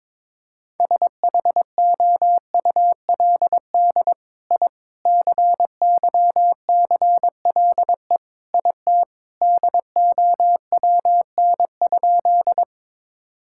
Morse code, heard as a keyed tone, about 22 wpm, 700 Hz. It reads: SHOULD I CYCLE IT DOWN?
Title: Sentences from Top 1000 Words - Review Part 6 - 22wpm